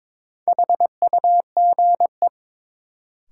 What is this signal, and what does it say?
Morse code, heard as a keyed tone, about 22 wpm, 700 Hz. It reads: HUGE
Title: Sentences from Top 1100 Words - New Word Drill - 22wpm